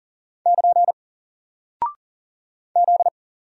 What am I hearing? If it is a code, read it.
Morse code, heard as a keyed tone, about 40 wpm, 700 Hz. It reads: <KN> Z
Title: Single Character (Letter:Number:Punctuation:Prosign) 40wpm